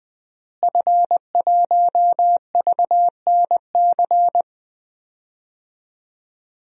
Morse code, heard as a keyed tone, about 21 wpm, 700 Hz. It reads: F1VNC
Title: DXCC-100 Call Signs - 5 Characters - Repeated 3x - 20wpm